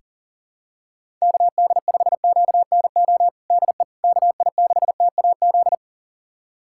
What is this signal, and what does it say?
Morse code, heard as a keyed tone, about 40 wpm, 700 Hz. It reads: KD5QNO DE KI6TAZ
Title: USA Callsign DE Callsign 40wpm